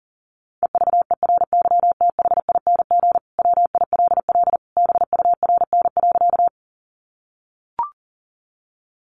Morse code, heard as a keyed tone, about 40 wpm, 700 Hz. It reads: EVERYTHING WILL BURN.
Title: Sentences from Top 1300 Words 40wpm